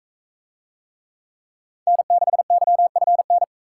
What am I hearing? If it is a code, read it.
Morse code, heard as a keyed tone, about 36 wpm, 700 Hz. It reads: N6YFN